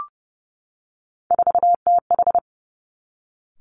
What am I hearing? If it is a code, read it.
Morse code, heard as a keyed tone, about 32 wpm, 700 Hz. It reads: 4TH
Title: Three Challenging Letter-Number Combinations 30wpm